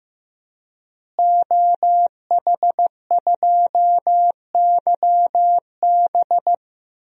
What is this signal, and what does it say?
Morse code, heard as a keyed tone, about 15 wpm, 700 Hz. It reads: OH2YB
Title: DXCC-100 Call Signs - 5 Characters - Repeated 2x - 15wpm